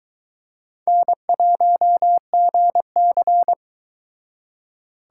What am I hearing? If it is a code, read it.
Morse code, heard as a keyed tone, about 24 wpm, 700 Hz. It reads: N1GC